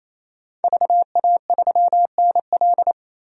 Morse code, heard as a keyed tone, about 27 wpm, 700 Hz. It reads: VA3NL
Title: DXCC-100 Call Signs - 5 Characters - Repeated 3x - 28wpm